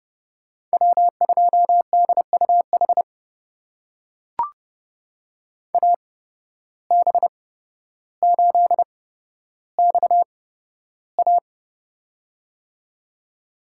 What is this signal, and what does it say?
Morse code, heard as a keyed tone, about 31 wpm, spaced as for 13 wpm, 700 Hz. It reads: W2DUH AB8XA
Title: USA Call Signs - 5 Characters - 8x Character Spacing - 30wpm